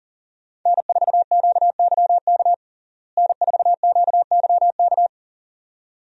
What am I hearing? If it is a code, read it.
Morse code, heard as a keyed tone, about 40 wpm, 700 Hz. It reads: N4QYK N4QYK
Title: USA Call Signs - 5 Characters - Repeated 2x - 40wpm